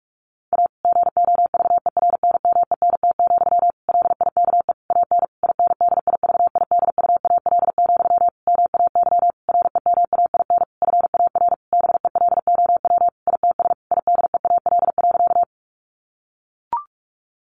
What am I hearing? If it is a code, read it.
Morse code, heard as a keyed tone, about 44 wpm, 700 Hz. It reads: A GOVERNMENT, LIKE AN INDIVIDUAL, MAY REMAIN FAR BELOW ITS IDEAL.
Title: Sentences from Top 1400 Words 45wpm